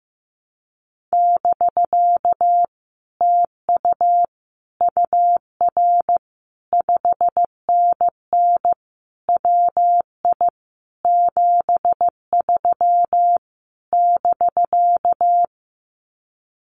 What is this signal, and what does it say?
Morse code, heard as a keyed tone, about 15 wpm, 700 Hz. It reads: <BK> TU UR 5NN WI 73 <BK>
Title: Common QSO Phrases Encore 15wpm